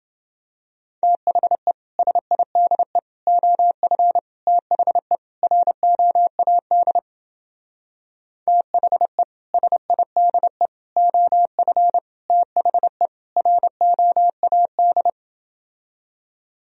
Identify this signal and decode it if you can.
Morse code, heard as a keyed tone, about 28 wpm, 700 Hz. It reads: THE SIDE OF THE ROAD THE SIDE OF THE ROAD